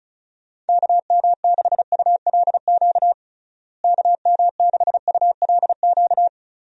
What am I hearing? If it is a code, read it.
Morse code, heard as a keyed tone, about 36 wpm, 700 Hz. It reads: KM6ULQ KM6ULQ
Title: USA Call Signs - 6 Characters - Repeated 3x - 35wpm